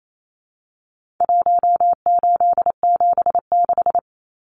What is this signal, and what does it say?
Morse code, heard as a keyed tone, about 28 wpm, 700 Hz. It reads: 1876
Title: Numbers - Contest Style - 4-Digit 28wpm